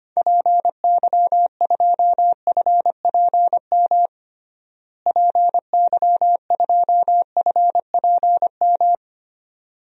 Morse code, heard as a keyed tone, about 25 wpm, 700 Hz. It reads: PY2FPM PY2FPM
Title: DXCC-100 Call Signs - 6 Characters - Repeated 2x - 25wpm